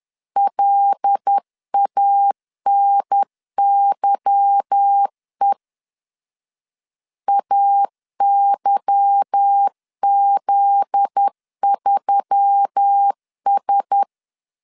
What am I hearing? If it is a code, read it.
Morse code, heard as a keyed tone, about 10 wpm, 800 Hz. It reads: LANYE AYZ3S
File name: LEZIONI / 8 YJ45 / 8 Y 10wpm 2615